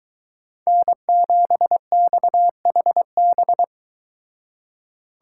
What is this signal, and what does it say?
Morse code, heard as a keyed tone, about 24 wpm, 700 Hz. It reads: N7XHB